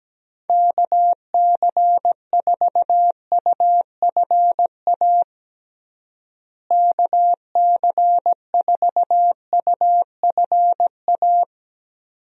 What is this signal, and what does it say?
Morse code, heard as a keyed tone, about 17 wpm, 700 Hz. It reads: KC4UFA KC4UFA
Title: USA Call Signs - 6 Characters - Repeated 2x - 17wpm